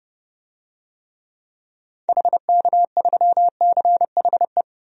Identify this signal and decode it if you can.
Morse code, heard as a keyed tone, about 30 wpm, 700 Hz. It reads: HK3CHE
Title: DXCC-100 Call Signs - 6 Characters - 30wpm